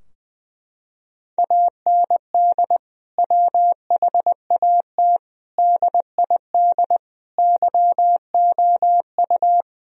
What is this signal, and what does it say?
Morse code, heard as a keyed tone, about 20 wpm, 700 Hz. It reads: AND WHAT DID YOU
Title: Sentences from Top 1000 Words - Review Part 3 - 20wpm